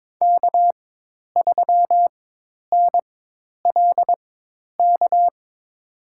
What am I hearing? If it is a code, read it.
Morse code, heard as a keyed tone, about 22 wpm, spaced as for 10 wpm, 700 Hz. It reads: K3NLK
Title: USA Call Signs - 5 Characters - 4x Character Spacing - 22wpm